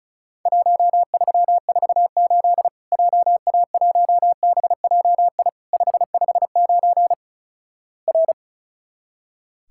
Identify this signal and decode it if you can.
Morse code, heard as a keyed tone, about 34 wpm, 700 Hz. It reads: 1348 JA1BJI 559 R
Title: Worked All Europe DX Contest - QTC Exchange - 35wpm